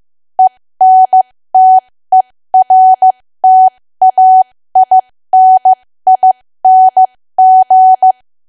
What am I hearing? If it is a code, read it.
Morse code, heard as a keyed tone, about 15 wpm, spaced as for 13 wpm, 750 Hz. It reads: ENTERTAINING
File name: cwa / W1AW-Level2 / 13WPM